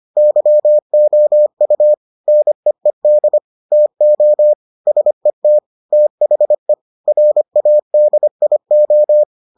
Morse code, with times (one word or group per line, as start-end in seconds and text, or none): 0.2-1.9 YOU
2.3-3.4 NEED
3.7-4.5 TO
4.9-5.6 SET
5.9-6.7 THE
7.1-9.2 RADIO